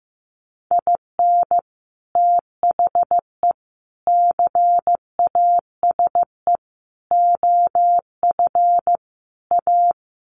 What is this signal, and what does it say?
Morse code, heard as a keyed tone, about 15 wpm, 700 Hz. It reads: IN THE CASE OF A
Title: Sets of 5 Words Encore 15wpm